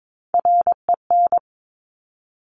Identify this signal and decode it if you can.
Morse code, heard as a keyed tone, about 22 wpm, 700 Hz. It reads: REN
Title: Top Three Letter Combinations in Words 22wpm